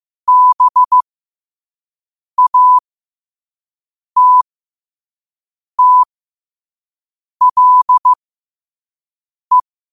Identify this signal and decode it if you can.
Morse code, heard as a keyed tone, about 15 wpm, spaced as for 5 wpm, 1000 Hz. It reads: BATTLE
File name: 5wpm